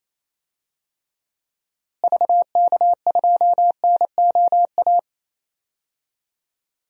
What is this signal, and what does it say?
Morse code, heard as a keyed tone, about 27 wpm, 700 Hz. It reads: VK2NOA
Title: DXCC-100 Call Signs - 6 Characters - Repeated 2x - 28wpm